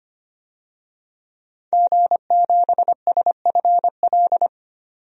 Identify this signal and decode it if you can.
Morse code, heard as a keyed tone, about 25 wpm, 700 Hz. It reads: G7SFL
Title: DXCC-100 Call Signs - 5 Characters - 25wpm